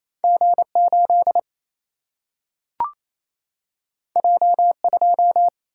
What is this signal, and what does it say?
Morse code, heard as a keyed tone, about 28 wpm, 700 Hz. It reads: G 8 J 2
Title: Single-Letter Single-Number 28wpm